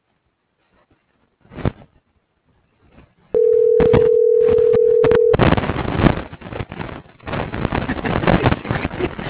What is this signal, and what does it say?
Ringback tone (North America), 440 + 480 Hz.